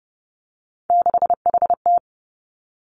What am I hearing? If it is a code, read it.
Morse code, heard as a keyed tone, about 30 wpm, 700 Hz. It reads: 6HT
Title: Three Challenging Letter-Number Combinations 30wpm